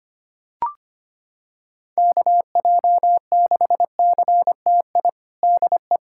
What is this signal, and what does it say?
Morse code, heard as a keyed tone, about 25 wpm, 700 Hz. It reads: KJ6CTI DE
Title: USA Callsign DE Callsign - 6 Characters - 25wpm